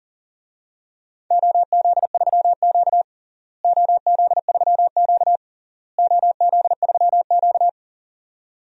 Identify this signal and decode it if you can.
Morse code, heard as a keyed tone, about 40 wpm, 700 Hz. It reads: OZ3Q OZ3Q OZ3Q
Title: DXCC-100 Call Signs - 4 Characters - Repeated 3x - 40wpm